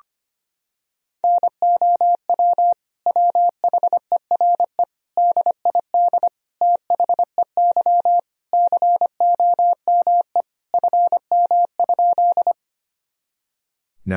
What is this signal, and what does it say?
Morse code, heard as a keyed tone, about 25 wpm, 700 Hz. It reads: NOW WHERE DID THEY COME FM?
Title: Sentences from Top 200 Words and Abbreviations 25wpm